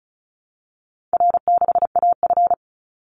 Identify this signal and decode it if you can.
Morse code, heard as a keyed tone, about 34 wpm, 700 Hz. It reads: R6AF
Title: Easy Call Signs Repeated 3x - 35wpm